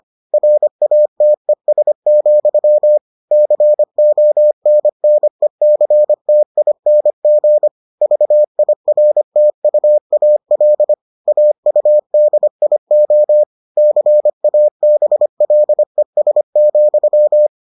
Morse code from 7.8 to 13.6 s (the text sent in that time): VIRTUAL AUDIO